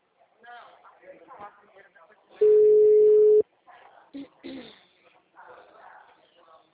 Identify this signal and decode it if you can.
Ringback tone (Europe, 425 Hz), with speech.
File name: out- -2019